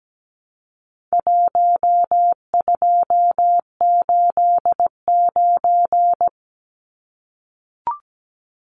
Morse code, heard as a keyed tone, about 17 wpm, 700 Hz. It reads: 1289
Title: Numbers - Contest Style - 4-Digit 17wpm